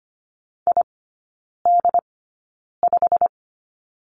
Morse code, heard as a keyed tone, about 25 wpm, spaced as for 9 wpm, 690 Hz. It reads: ID5
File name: Lesson 15e - Number 5 - Three Characters 25wpm